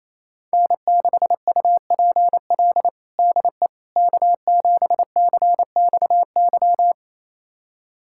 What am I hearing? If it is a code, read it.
Morse code, heard as a keyed tone, about 28 wpm, 700 Hz. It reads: N6UPL DE K7CXY
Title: USA Callsign DE Callsign - 5 Characters - 28wpm